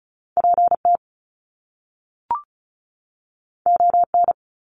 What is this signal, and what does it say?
Morse code, heard as a keyed tone, about 36 wpm, 700 Hz. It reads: PT ON
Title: Top Two Letter Combinations in Words 35wpm